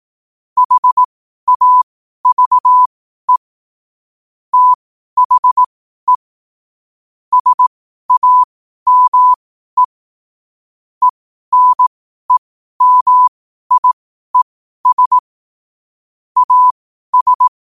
Morse code, heard as a keyed tone, about 18 wpm, spaced as for 12 wpm, 1000 Hz. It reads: HAVE THE SAME ENEMIES AS
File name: 12 wpm